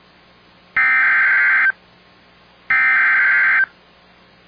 Emergency Alert System (SAME) tones (header bursts), heard as harsh buzzing bursts. It is heard from 0.8 to 3.6 s.